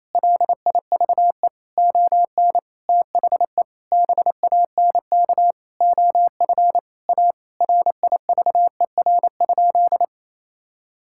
Morse code, heard as a keyed tone, about 28 wpm, 700 Hz. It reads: LIVE ON THE BANK OF A RIVER?
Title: Sentences from Top 1000 Words - Review Part 6 - 28wpm